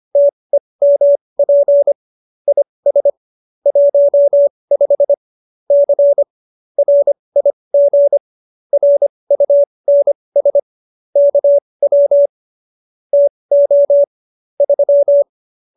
Morse code, heard as a keyed tone, about 25 wpm, spaced as for 20 wpm, 570 Hz. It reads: TEMP IS 15 C RIG RUNS KW TO 3
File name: QSO309 20